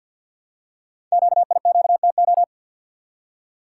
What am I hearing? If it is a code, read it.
Morse code, heard as a keyed tone, about 50 wpm, 690 Hz. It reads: QIQTO
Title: Code Groups - 5 Letters - 50wpm